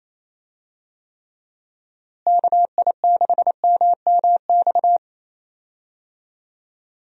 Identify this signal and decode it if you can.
Morse code, heard as a keyed tone, about 27 wpm, 700 Hz. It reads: KI6MMX
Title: USA Call Signs - 6 Characters - Repeated 3x - 28wpm